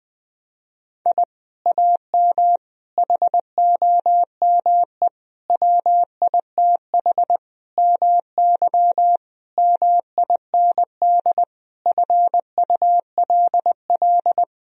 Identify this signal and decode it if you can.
Morse code, heard as a keyed tone, about 21 wpm, 700 Hz. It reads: I AM HOME WITH MY MIND FULL
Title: Sentences from Top 1000 Words - Review Part 3 - 20wpm